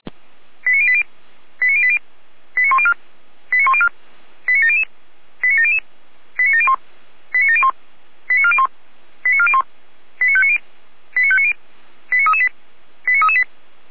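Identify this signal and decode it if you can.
Five-tone selective call (ZVEI), heard as quick runs of five tones, tones of about 70 ms. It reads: 89080 89080 89104 89104 89700 89700 89701 89701 89401 89401 89590 89590 89338 89338
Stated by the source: ZVEI five-tone selective call, code 89340, received by SDR.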